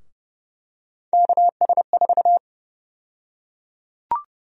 Morse code, heard as a keyed tone, about 31 wpm, 700 Hz. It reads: KS4
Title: USA Call Sign Prefixes - 30wpm